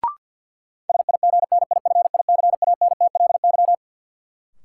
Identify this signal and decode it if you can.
Morse code, heard as a keyed tone, about 46 wpm, 700 Hz. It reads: SIGNIFICANTLY
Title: Sentences from Top 1500 Words - New Word Drill - 50wpm